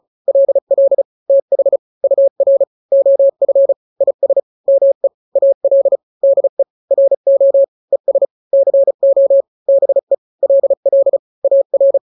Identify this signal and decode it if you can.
Morse code, heard as a keyed tone, about 34 wpm, 550 Hz. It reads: LL TH UR OF IS ME AL DE RO ES CO BE LL AR
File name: n-grams / 2-grams 35 30 WPM